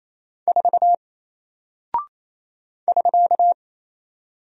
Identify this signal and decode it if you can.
Morse code, heard as a keyed tone, about 27 wpm, 700 Hz. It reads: 4 <SK>
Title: Single Character (Letter:Number:Punctuation:Prosign) Alt 28wpm